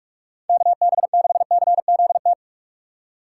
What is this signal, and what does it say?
Morse code, heard as a keyed tone, about 46 wpm, 700 Hz. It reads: KB6CZT
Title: USA Call Signs - 6 Characters - 4x Character Spacing - 45wpm